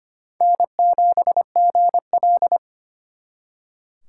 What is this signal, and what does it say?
Morse code, heard as a keyed tone, about 32 wpm, 700 Hz. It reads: N7GL